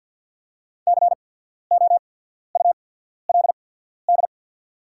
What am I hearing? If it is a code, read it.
Morse code, heard as a keyed tone, about 50 wpm, spaced as for 14 wpm, 700 Hz. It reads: COULD
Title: 8x Character Spacing - Top 100 Words - 50wpm